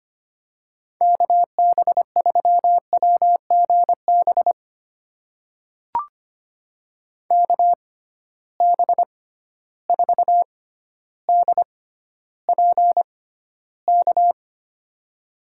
Morse code, heard as a keyed tone, about 25 wpm, spaced as for 13 wpm, 700 Hz. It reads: KB3WGB KB4DPK
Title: USA Call Signs - 6 Characters - 6x Character Spacing - 25wpm